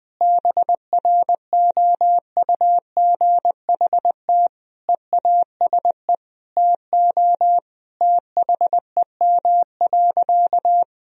Morse code, heard as a keyed tone, about 20 wpm, 700 Hz. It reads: BROUGHT EASE TO THEM.
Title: Sentences from Top 1000 Words - Review Part 1 - 20wpm